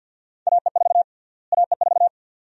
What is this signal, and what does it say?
Morse code, heard as a keyed tone, about 52 wpm, 700 Hz. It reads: AE4 AE4
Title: USA Call Sign Prefixes - Repeated 2x - 50wpm